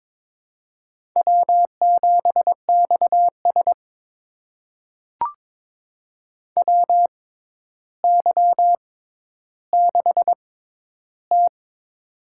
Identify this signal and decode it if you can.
Morse code, heard as a keyed tone, about 21 wpm, spaced as for 11 wpm, 700 Hz. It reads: W7XS WY6T
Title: USA Call Signs - 4 Characters - 6x Character Spacing - 22wpm